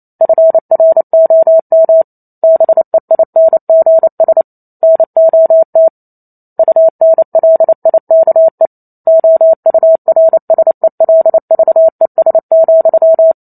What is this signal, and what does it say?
Morse code, heard as a keyed tone, about 28 wpm, 650 Hz. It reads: FROM BEINGS NOT UNLIKE OURSELVES,